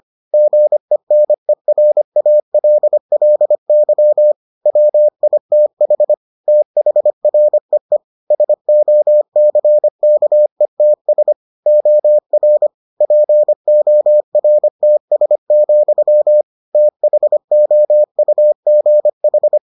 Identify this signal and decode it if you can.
Morse code, heard as a keyed tone, about 25 wpm, 600 Hz. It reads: GENERALLY WITH THREE SOCKETS OR PORTS, THOUGH